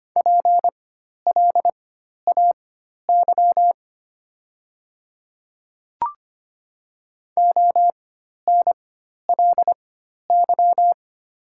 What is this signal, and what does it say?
Morse code, heard as a keyed tone, about 25 wpm, spaced as for 12 wpm, 700 Hz. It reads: PLAY ONLY
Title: 4x Character Spacing - Top 200 Words - 25wpm